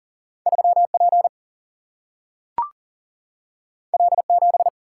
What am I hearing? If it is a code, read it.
Morse code, heard as a keyed tone, about 40 wpm, 700 Hz. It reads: 3 P L 7
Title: Single-Letter Single-Number 40wpm